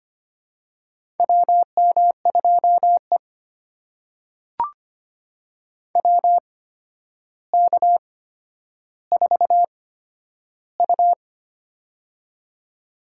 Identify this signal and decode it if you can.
Morse code, heard as a keyed tone, about 25 wpm, spaced as for 11 wpm, 700 Hz. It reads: WM2E WK4U
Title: USA Call Signs - 4 Characters - 8x Character Spacing - 25wpm